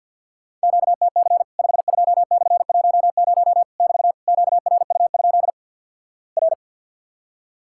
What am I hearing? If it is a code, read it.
Morse code, heard as a keyed tone, about 50 wpm, 700 Hz. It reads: QTC 52/10 <BT> QRU? R
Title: Worked All Europe DX Contest - QTC Exchange - 50wpm